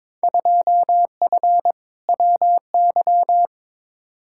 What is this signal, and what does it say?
Morse code, heard as a keyed tone, about 21 wpm, 700 Hz. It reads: 2F WY
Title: Field Day Exchanges Encore - 22wpm